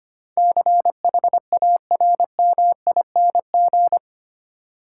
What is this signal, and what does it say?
Morse code, heard as a keyed tone, about 25 wpm, 700 Hz. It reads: CHARMING
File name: 8 Letter Words - First 3 Letters Given - 25wpm